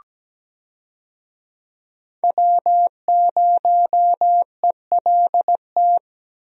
Morse code, heard as a keyed tone, about 17 wpm, 700 Hz. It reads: W0ELT
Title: USA Call Signs - 5 Characters - 17wpm